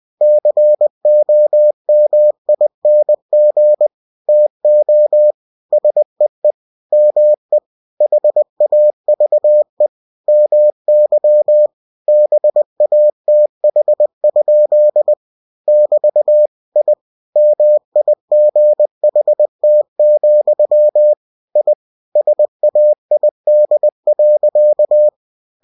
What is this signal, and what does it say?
Morse code, heard as a keyed tone, about 20 wpm, 600 Hz. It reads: COMING TO SEE ME HAVE MY BATH? <BT> I MIGHT, I SAID.